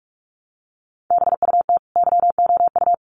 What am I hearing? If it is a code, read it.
Morse code, heard as a keyed tone, about 44 wpm, 700 Hz. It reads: BUT YOU
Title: Sets of 2 Words 45wpm